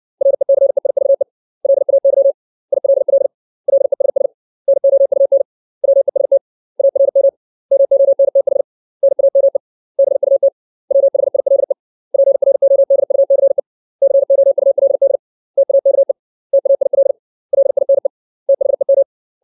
Morse code, heard as a keyed tone, about 60 wpm, 550 Hz. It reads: RECEIVE LAY ILL LESS NOUN WEST ARM MONTH NAME BUT WHILE PRODUCE COULD TAKE TAIL LINE THEM
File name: 632 words / 632 Words 60WPM